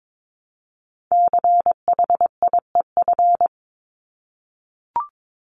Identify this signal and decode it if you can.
Morse code, heard as a keyed tone, about 22 wpm, 700 Hz. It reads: CHIEF
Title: LY Suffix - Top 100 Words with Base Words 22wpm